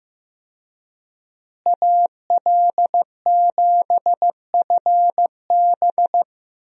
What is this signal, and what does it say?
Morse code, heard as a keyed tone, about 15 wpm, 700 Hz. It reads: AL7FB
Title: USA Call Signs - 5 Characters - Repeated 2x - 15wpm